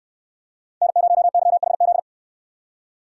Morse code, heard as a keyed tone, about 69 wpm, 700 Hz. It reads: N0YHZ